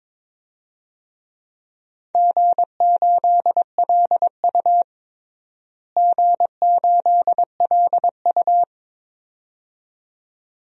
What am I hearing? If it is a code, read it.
Morse code, heard as a keyed tone, about 21 wpm, 700 Hz. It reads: G8LU G8LU
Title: DXCC-100 Call Signs - 4 Characters - Repeated 2x - 22wpm